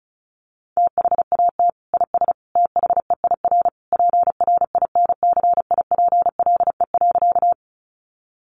Morse code, heard as a keyed tone, about 34 wpm, 700 Hz. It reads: THAT IS THEIR PRINCIPLE.
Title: Sentences from Top 1400 Words 35wpm